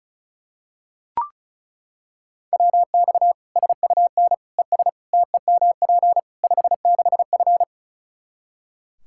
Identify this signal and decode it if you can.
Morse code, heard as a keyed tone, about 34 wpm, 700 Hz. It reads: WX SUN ES TEMP 56F
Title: Practice QSO - Exchange 2 - Info - 35wpm